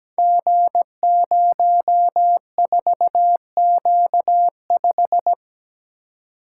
Morse code, heard as a keyed tone, about 17 wpm, 700 Hz. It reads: G04Q5
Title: Code Groups - 5 Letter:Numbers - 17wpm